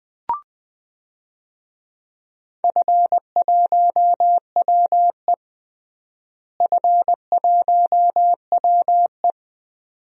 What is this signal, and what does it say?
Morse code, heard as a keyed tone, about 20 wpm, 700 Hz. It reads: F1WE F1WE
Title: DXCC-100 Call Signs - 4 Characters - Repeated 2x - 20wpm